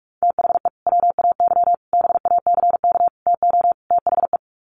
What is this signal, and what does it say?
Morse code, heard as a keyed tone, about 44 wpm, 700 Hz. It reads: THE WAY BACK TO THE
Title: Sets of 5 Words Encore2 45wpm